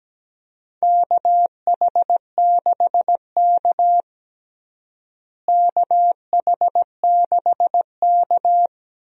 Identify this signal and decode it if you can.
Morse code, heard as a keyed tone, about 17 wpm, 700 Hz. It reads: KH6K KH6K
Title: DXCC-100 Call Signs - 4 Characters - Repeated 3x - 17wpm